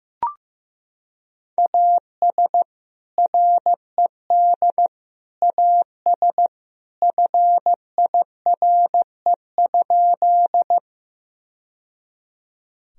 Morse code, heard as a keyed tone, about 15 wpm, 700 Hz. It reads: AS RED AS FIRE?
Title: Sentences from Top 1000 Words - Review Part 2 - 15wpm